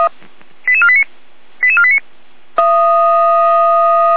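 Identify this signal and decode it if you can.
Five-tone selective call (ZVEI), heard as quick runs of five tones, tones of about 70 ms. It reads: 88489 88489
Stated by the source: ZVEI five-tone selective call, code 88216, received via SDR.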